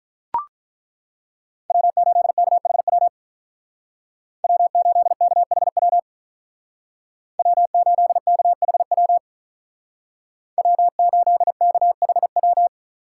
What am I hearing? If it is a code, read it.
Morse code, heard as a keyed tone, about 43 wpm, 700 Hz. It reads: W8KHW W8KHW W8KHW W8KHW
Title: Speed Racing - USA Call Signs 35wpm